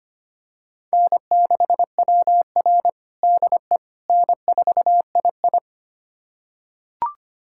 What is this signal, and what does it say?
Morse code, heard as a keyed tone, about 25 wpm, 700 Hz. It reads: N6WR DE N4II
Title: USA Callsign DE Callsign - 4 Characters - 25wpm